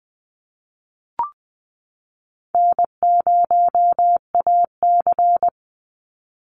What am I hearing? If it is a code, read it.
Morse code, heard as a keyed tone, about 20 wpm, 700 Hz. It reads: N0AC
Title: Easy Call Signs Repeated 2x - 20wpm